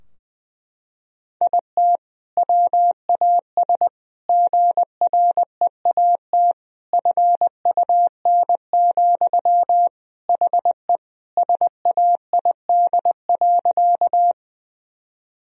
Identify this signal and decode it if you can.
Morse code, heard as a keyed tone, about 20 wpm, 700 Hz. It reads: IT WAS GREAT FUN, HE SAID.
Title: Sentences from Top 1000 Words - Review Part 4 - 20wpm